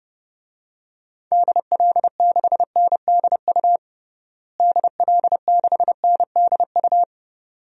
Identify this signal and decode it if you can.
Morse code, heard as a keyed tone, about 30 wpm, 700 Hz. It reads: DL6NDU DL6NDU
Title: DXCC-100 Call Signs - 6 Characters - Repeated 2x - 30wpm